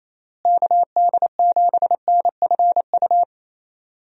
Morse code, heard as a keyed tone, about 28 wpm, 700 Hz. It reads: KD7NFU
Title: USA Call Signs - 6 Characters - Repeated 2x - 28wpm